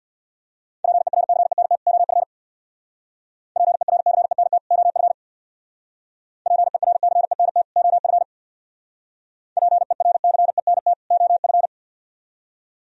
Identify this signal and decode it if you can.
Morse code, heard as a keyed tone, about 62 wpm, 700 Hz. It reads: PERCENT OF PERCENT OF PERCENT OF PERCENT OF
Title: Speed-Racing - Sets of 2 Words 50wpm